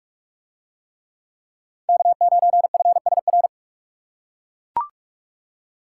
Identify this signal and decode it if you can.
Morse code, heard as a keyed tone, about 40 wpm, 700 Hz. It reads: K9FSR